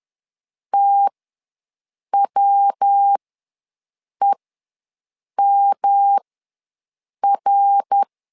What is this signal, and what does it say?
Morse code, heard as a keyed tone, about 11 wpm, spaced as for 6 wpm, 800 Hz. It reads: TWEMR
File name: LEZIONI / 3 MWRO / 3 MWR ws 10wpm 2151